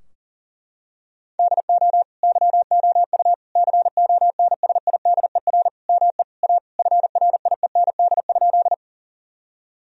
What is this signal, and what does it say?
Morse code, heard as a keyed tone, about 40 wpm, 700 Hz. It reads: DO YOU CONSIDER ME A FRIEND?
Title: Sentences from Top 1000 Words - Review Part 4 - 40wpm